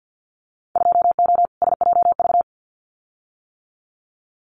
Morse code, heard as a keyed tone, about 50 wpm, 700 Hz. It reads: 2O SJV
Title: Winter Field Day Exchanges Encore 50wpm